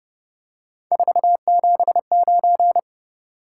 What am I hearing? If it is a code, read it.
Morse code, heard as a keyed tone, about 30 wpm, 700 Hz. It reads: 479
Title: Top 180 QSO Elements 30wpm